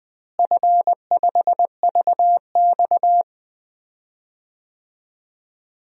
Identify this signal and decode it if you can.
Morse code, heard as a keyed tone, about 21 wpm, 700 Hz. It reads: F5VX
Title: DXCC-100 Call Signs - 4 Characters - Repeated 3x - 20wpm